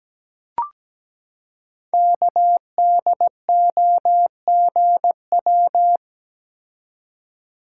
Morse code, heard as a keyed tone, about 17 wpm, 700 Hz. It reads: KDOGW
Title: Code Groups - 5 Letters - 17wpm